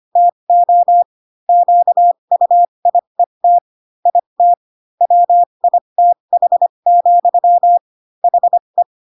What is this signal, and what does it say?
Morse code, heard as a keyed tone, about 25 wpm, spaced as for 22 wpm, 700 Hz. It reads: TO QUIET IT WITH, HE